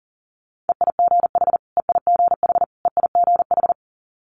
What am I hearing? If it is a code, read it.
Morse code, heard as a keyed tone, about 40 wpm, 700 Hz. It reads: EIGH EIGH EIGH
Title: Multi-letter Phonemes Repeated 3x 40wpm